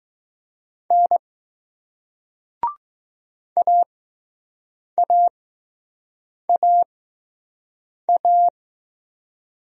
Morse code, heard as a keyed tone, about 20 wpm, spaced as for 6 wpm, 700 Hz. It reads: N AAAA